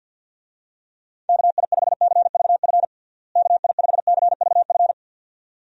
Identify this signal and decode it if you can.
Morse code, heard as a keyed tone, about 53 wpm, 700 Hz. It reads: KI5CVF KI5CVF
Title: USA Call Signs - 6 Characters - Repeated 2x - 50wpm